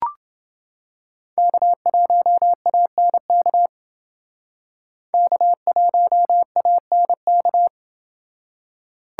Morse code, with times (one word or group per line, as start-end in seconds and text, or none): 1.4-3.7 K1ANK
5.1-7.7 K1ANK